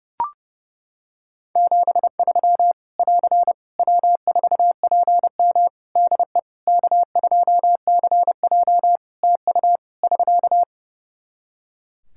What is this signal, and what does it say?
Morse code, heard as a keyed tone, about 30 wpm, 700 Hz. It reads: 73 <AR> W4PM DE K2CJ TU <SK>
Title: K1USN SST QSO Practice 30wpm